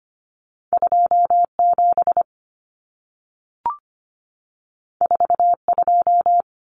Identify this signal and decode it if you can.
Morse code, heard as a keyed tone, about 25 wpm, 700 Hz. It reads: 27 42
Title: Common Cut Numbers - 2-Digit 25wpm